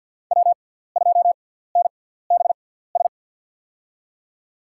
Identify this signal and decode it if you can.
Morse code, heard as a keyed tone, about 50 wpm, spaced as for 17 wpm, 700 Hz. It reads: W2NBS
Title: USA Call Signs - 5 Characters - 6x Character Spacing - 50wpm